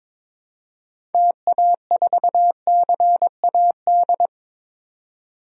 Morse code, heard as a keyed tone, about 22 wpm, 700 Hz. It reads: TA4CAD
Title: DXCC-100 Call Signs - 6 Characters - 22wpm